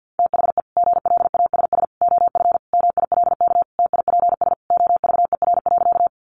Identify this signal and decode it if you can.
Morse code, heard as a keyed tone, about 50 wpm, 700 Hz. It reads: THE GLASS OF MILK TIPS OVER.
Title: Additional Easy Phrases 50wpm